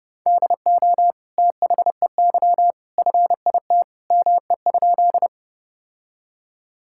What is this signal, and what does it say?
Morse code, heard as a keyed tone, about 30 wpm, 700 Hz. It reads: DO THEY FIT ME?
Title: Sentences from Top 1000 Words - Review Part 5 - 30wpm